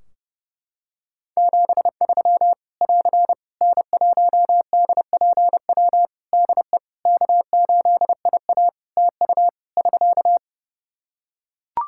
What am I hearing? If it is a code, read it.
Morse code, heard as a keyed tone, about 30 wpm, 700 Hz. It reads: 73 <AR> N1DPW DE K8IA TU <SK>
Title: K1USN SST QSO Practice 30wpm